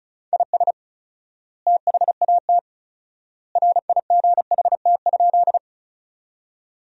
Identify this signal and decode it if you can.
Morse code, heard as a keyed tone, about 34 wpm, 700 Hz. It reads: IS THAT RIGHT?
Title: 4x Word-Spacing - Sentences from Top 200 Words - 35wpm